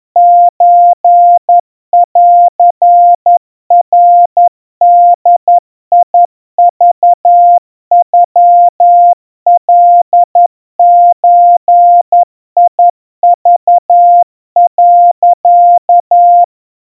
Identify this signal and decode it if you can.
Morse code (Japanese Wabun code), heard as a keyed tone, about 11 wpm, 700 Hz. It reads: ソンナボクノカゾク、
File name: 40 cpm 700 Hz Crispy